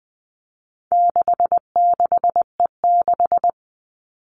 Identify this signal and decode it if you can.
Morse code, heard as a keyed tone, about 20 wpm, 700 Hz. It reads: 66E6
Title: All Cut Numbers - 4-Digit 20wpm